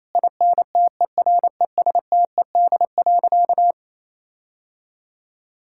Morse code, heard as a keyed tone, about 28 wpm, 700 Hz. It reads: INTERESTED.
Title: Sentences from Top 1000 Words - Review Part 6 - 28wpm